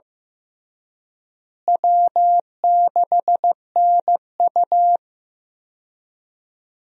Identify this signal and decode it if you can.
Morse code, heard as a keyed tone, about 15 wpm, 700 Hz. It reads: W6NU
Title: USA Call Signs - 4 Characters - Repeated 3x - 15wpm